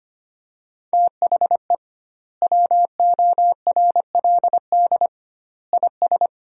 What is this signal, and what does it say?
Morse code, heard as a keyed tone, about 25 wpm, 700 Hz. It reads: THE WORLD IS
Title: 2x Word-Spacing - Sentences from Top 400 Words - 25wpm